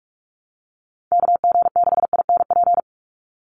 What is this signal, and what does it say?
Morse code, heard as a keyed tone, about 44 wpm, 700 Hz. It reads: KG6INP
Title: USA Call Signs 45wpm